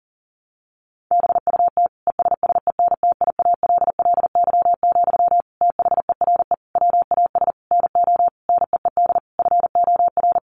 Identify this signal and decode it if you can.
Morse code, heard as a keyed tone, about 40 wpm, 700 Hz. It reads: BUT ESSENTIALLY, THERE WAS NO NEED FOR
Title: Sentences from Top 1300 Words 40wpm